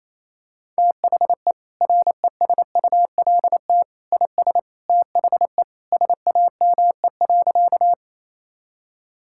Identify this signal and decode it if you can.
Morse code, heard as a keyed tone, about 28 wpm, 700 Hz. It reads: THE RESULT IS THE SAME.
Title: Sentences from Top 1000 Words - Review Part 5 - 28wpm